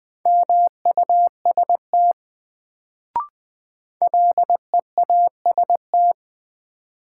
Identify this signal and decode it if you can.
Morse code, heard as a keyed tone, about 20 wpm, 700 Hz. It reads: MUST LEAST
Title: Sets of 2 Words - New Word Drill - 20wpm